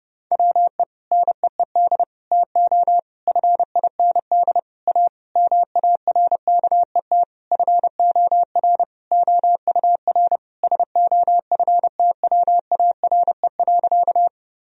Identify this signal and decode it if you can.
Morse code, heard as a keyed tone, about 30 wpm, 700 Hz. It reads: WE NEED TO FIND A MARKET FOR OUR SOFTWARE.
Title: Sentences from Top 1000 Words - Review Part 3 - 30wpm